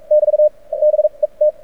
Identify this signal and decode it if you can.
Morse code, heard as a keyed tone, about 44 wpm, spaced as for 33 wpm, 600 Hz. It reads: <BT> BET
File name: archived / CRSnet